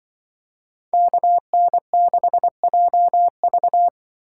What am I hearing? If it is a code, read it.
Morse code, heard as a keyed tone, about 23 wpm, 700 Hz. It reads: KN6JV
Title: Speed Racing - USA Call Signs 20wpm